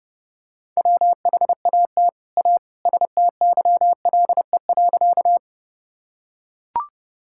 Morse code, heard as a keyed tone, about 30 wpm, 700 Hz. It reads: WHAT A STYLE.
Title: Sentences from Top 1000 Words - Review Part 5 - 30wpm